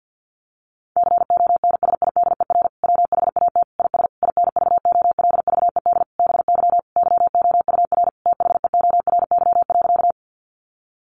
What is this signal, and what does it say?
Morse code, heard as a keyed tone, about 50 wpm, 700 Hz. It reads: CONSIDER WHAT IS INVOLVED BY YOUR THEORY.
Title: Sentences from Top 800 Words 50wpm